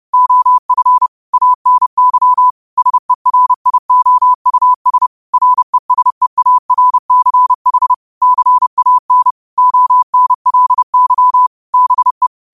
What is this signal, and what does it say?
Morse code, heard as a keyed tone, about 30 wpm, 1000 Hz. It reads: OF ANY SERIOUS RESEARCH CAN ONLY BE
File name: 30wpm